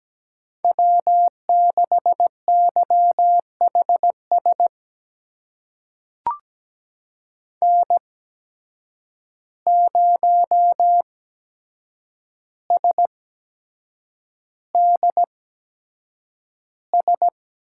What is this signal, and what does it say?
Morse code, heard as a keyed tone, about 17 wpm, spaced as for 7 wpm, 700 Hz. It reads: W6YHS N0SDS
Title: USA Call Signs - 5 Characters - 8x Character Spacing - 17wpm